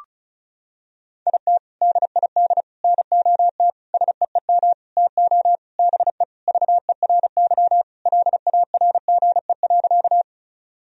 Morse code, heard as a keyed tone, about 36 wpm, 700 Hz. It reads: IT DID NOT SEEM TO BE VERY LARGE.
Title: Sentences from Top 1000 Words - Review Part 1 - 35wpm